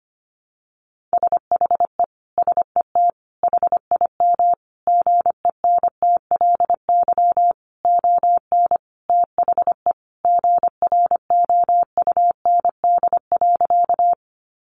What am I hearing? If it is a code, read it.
Morse code, heard as a keyed tone, about 25 wpm, 700 Hz. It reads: SHE SET HIM GENTLY ON THE GROUND.
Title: Sentences from Top 1500 Words 25wpm